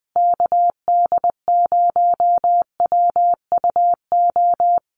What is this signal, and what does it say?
Morse code, heard as a keyed tone, about 20 wpm, 700 Hz. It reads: KD0WUO
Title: USA Call Signs Repeated 3x 20wpm